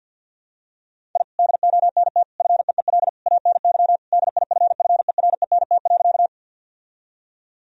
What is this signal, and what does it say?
Morse code, heard as a keyed tone, about 50 wpm, 700 Hz. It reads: I DONT FEEL ANY DIFFERENT.
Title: Sentences from Top 1000 Words - Review Part 3 - 50wpm